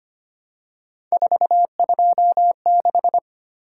Morse code, heard as a keyed tone, about 25 wpm, 700 Hz. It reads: 426
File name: Three Letter-Number Combinations - 25wpm